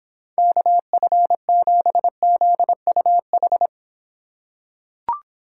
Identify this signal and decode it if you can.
Morse code, heard as a keyed tone, about 25 wpm, 700 Hz. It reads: KF7ZUH